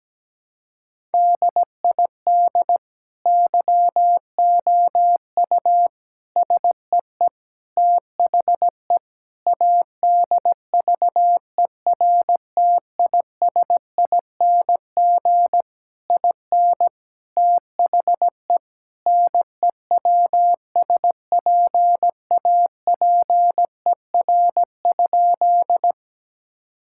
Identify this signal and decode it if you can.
Morse code, heard as a keyed tone, about 17 wpm, 700 Hz. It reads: DID YOU SEE THE ADVERTISING IN THE NEWSPAPER?
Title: Sentences from Top 1000 Words - Review Part 6 - 17wpm